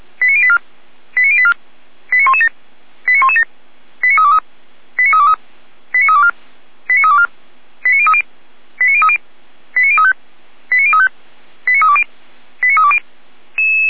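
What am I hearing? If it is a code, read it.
Five-tone selective call (ZVEI), heard as quick runs of five tones, tones of about 70 ms. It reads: 89084 89084 89117 89117 89323 89323 89324 89324 89030 89030 89035 89035 89320 89320
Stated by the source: ZVEI five-tone selective call 89501, received by SDR, fire brigade Feuerwehr Orlamünde.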